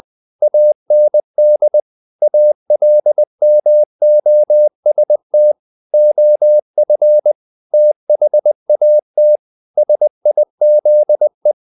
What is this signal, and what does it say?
Morse code, heard as a keyed tone, about 20 wpm, 600 Hz. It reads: AND ALMOST OF THAT SIZE
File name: the old man and the sea / the old man and the sea part 1 20wpm